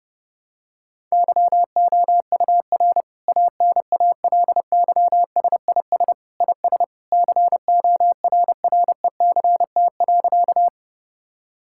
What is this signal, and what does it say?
Morse code, heard as a keyed tone, about 32 wpm, 700 Hz. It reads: YOUR ANALYSIS IS CORRECT.
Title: Sentences from Top 1000 Words - Review Part 5 - 30wpm